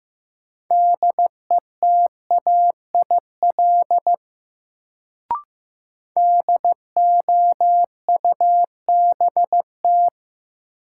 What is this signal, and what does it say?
Morse code, heard as a keyed tone, about 15 wpm, 700 Hz. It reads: DETAIL DOUBT
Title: Sentences from Top 1500 Words - New Word Drill - 15wpm